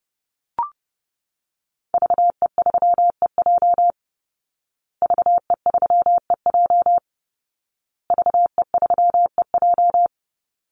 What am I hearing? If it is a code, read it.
Morse code, heard as a keyed tone, about 30 wpm, 700 Hz. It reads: VE3EJ VE3EJ VE3EJ
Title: Call Signs Repeated 2x - 30wpm